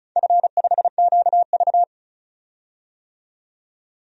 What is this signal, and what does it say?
Morse code, heard as a keyed tone, about 36 wpm, 700 Hz. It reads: F5QV
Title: DXCC-100 Call Signs - 4 Characters - 35wpm